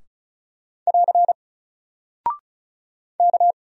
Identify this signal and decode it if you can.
Morse code, heard as a keyed tone, about 36 wpm, 700 Hz. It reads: <AR> K
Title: Single Character (Letter:Number:Punctuation:Prosign) 35wpm